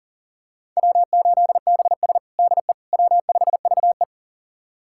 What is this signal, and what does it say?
Morse code, heard as a keyed tone, about 40 wpm, 700 Hz. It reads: W8BS DE W5VE
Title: USA Callsign DE Callsign - 4 Characters - 40wpm